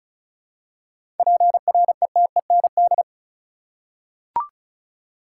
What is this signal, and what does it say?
Morse code, heard as a keyed tone, about 36 wpm, 700 Hz. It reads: PRETEND
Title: Seven Letter Words - 35wpm